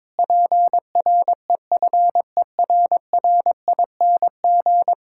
Morse code, heard as a keyed tone, about 22 wpm, 700 Hz. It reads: PREFERRING
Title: Pangram Challenge v2 - 22wpm